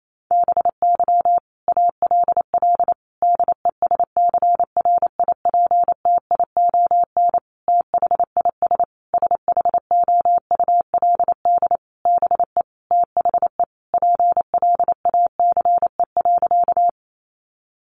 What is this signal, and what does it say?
Morse code, heard as a keyed tone, about 28 wpm, 700 Hz. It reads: BY ALL DESCRIPTION THIS SHOULD BE THE PLACE.
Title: Sentences from Top 1300 Words 28wpm